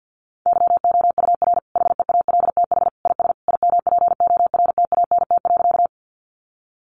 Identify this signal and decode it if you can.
Morse code, heard as a keyed tone, about 48 wpm, 700 Hz. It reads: YOUR HEALTH IS IMPORTANT.
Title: Sentences from Top 600 Words 50wpm